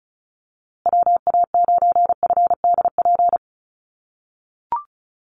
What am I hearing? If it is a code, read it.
Morse code, heard as a keyed tone, about 34 wpm, 700 Hz. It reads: WA9FDP
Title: USA Call Signs Repeated 2x 35wpm